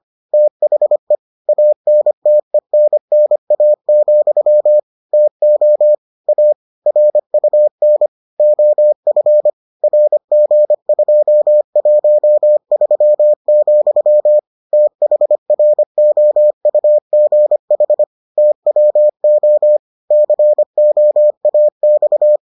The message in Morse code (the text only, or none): THE ANTENNA, TO A RUN OF RG213, THROUGH TWO COAX